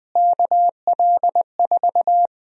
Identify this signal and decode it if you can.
Morse code, heard as a keyed tone, about 20 wpm, 700 Hz. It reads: KL4
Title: USA Call Sign Prefixes - Repeated 2x - 20wpm